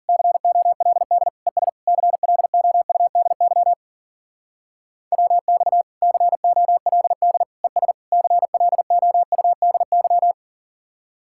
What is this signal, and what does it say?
Morse code, heard as a keyed tone, about 47 wpm, 700 Hz. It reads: COLD ES CLOUDY WX COLD ES CLOUDY
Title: Speed-Racing Common QSO Phrases Encore - 40wpm